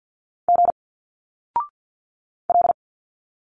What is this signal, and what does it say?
Morse code, heard as a keyed tone, about 50 wpm, 700 Hz. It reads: G L
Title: Single Letters 50wpm